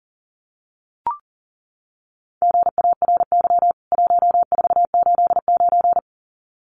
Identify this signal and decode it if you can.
Morse code, heard as a keyed tone, about 40 wpm, 700 Hz. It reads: GARY 1489
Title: CWT Exchange 40wpm